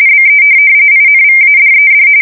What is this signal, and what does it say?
RTTY, about 45.45 baud. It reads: BE INVADED
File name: fsk170 / USS-New-Jersey